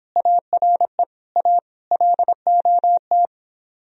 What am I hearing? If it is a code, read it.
Morse code, heard as a keyed tone, about 26 wpm, 700 Hz. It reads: ARE A LOT